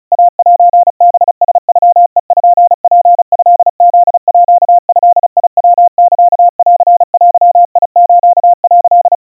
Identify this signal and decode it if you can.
Morse code (Japanese Wabun code), heard as a keyed tone, about 34 wpm, 700 Hz. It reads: イセハラノペツトフードヤサンデス」
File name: kawa / Morse sample / kamyu 130 cpm 700 Hz Crispy